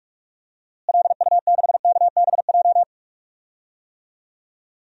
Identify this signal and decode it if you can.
Morse code, heard as a keyed tone, about 46 wpm, 700 Hz. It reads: LU6KBJ
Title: DXCC-100 Call Signs - 6 Characters - Repeated 3x - 45wpm